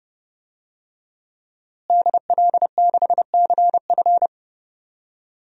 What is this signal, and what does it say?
Morse code, heard as a keyed tone, about 32 wpm, 700 Hz. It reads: DL6CF
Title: DXCC-100 Call Signs - 5 Characters - Repeated 2x - 30wpm